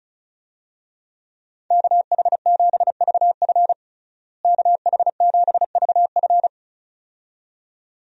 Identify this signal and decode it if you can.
Morse code, heard as a keyed tone, about 36 wpm, 700 Hz. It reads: KH7VF KH7VF
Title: USA Call Signs - 5 Characters - Repeated 2x - 35wpm